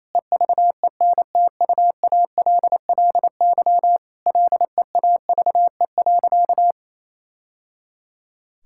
Morse code, heard as a keyed tone, about 27 wpm, 700 Hz. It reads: EVENTUALLY LEAVE.
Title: Sentences from Top 1000 Words - Review Part 6 - 28wpm